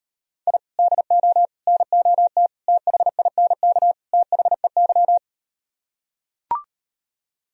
Morse code, heard as a keyed tone, about 38 wpm, 700 Hz. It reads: I DO NOT THINK THEY